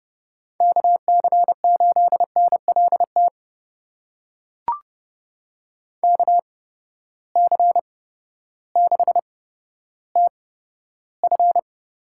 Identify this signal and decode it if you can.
Morse code, heard as a keyed tone, about 29 wpm, spaced as for 14 wpm, 700 Hz. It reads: KC8NLT KC6TF
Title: USA Call Signs - 6 Characters - 8x Character Spacing - 30wpm